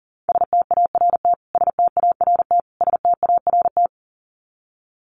Morse code, heard as a keyed tone, about 40 wpm, 700 Hz. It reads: START START START
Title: Top 500 Words Repeated 40wpm